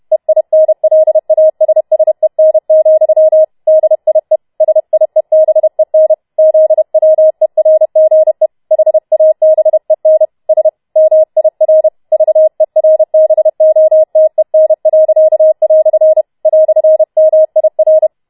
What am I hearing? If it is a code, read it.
Morse code, heard as a keyed tone, about 31 wpm, 630 Hz. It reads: EINLASSEN, DIE SIEBEN ZWERGE HABEN S MIR VERBOTEN." "MIR